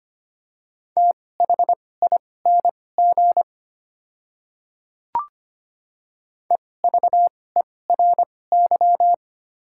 Morse code, heard as a keyed tone, about 25 wpm, spaced as for 18 wpm, 700 Hz. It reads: THING EVERY
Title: 2x Character Spacing - Top 200 Words - 25wpm